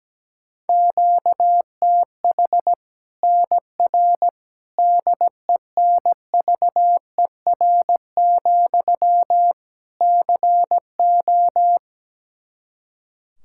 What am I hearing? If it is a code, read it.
Morse code, heard as a keyed tone, about 17 wpm, 700 Hz. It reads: QTH NR DENVER, CO
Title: K1USN SST QSO Practice 17wpm